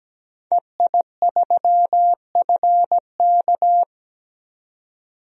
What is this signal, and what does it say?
Morse code, heard as a keyed tone, about 17 wpm, 700 Hz. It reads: EI3FK
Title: DXCC-100 Call Signs - 5 Characters - 17wpm